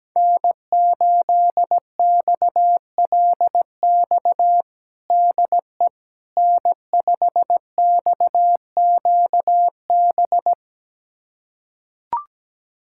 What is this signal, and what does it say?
Morse code, heard as a keyed tone, about 17 wpm, 700 Hz. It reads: N8XLX DE N5XQB
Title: USA Callsign DE Callsign - 5 Characters - 17wpm